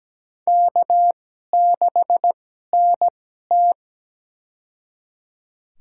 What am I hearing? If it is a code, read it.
Morse code, heard as a keyed tone, about 17 wpm, spaced as for 12 wpm, 700 Hz. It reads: K6NT
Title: USA Call Signs - 4 Characters - 2x Character Spacing - 17wpm